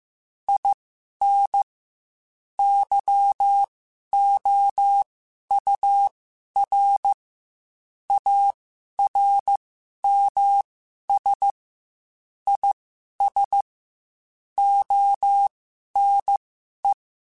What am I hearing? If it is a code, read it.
Morse code, heard as a keyed tone, about 15 wpm, spaced as for 11 wpm, 800 Hz. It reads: IN YOUR ARMS IS ONE
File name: Quotes / Quotes-mp3s / ChildAsleep @13